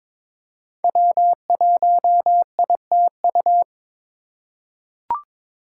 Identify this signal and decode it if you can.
Morse code, heard as a keyed tone, about 22 wpm, 700 Hz. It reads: W1ITU
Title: USA Call Signs - 5 Characters - 4x Character Spacing - 22wpm